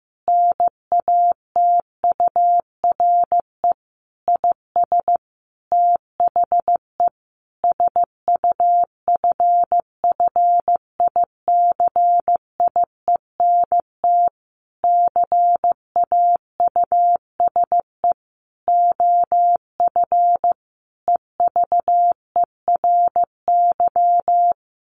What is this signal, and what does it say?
Morse code, heard as a keyed tone, about 15 wpm, 700 Hz. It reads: NATURE IS THE SUFFICIENT CAUSE OF EVERY
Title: Sentences from Top 1400 Words 15wpm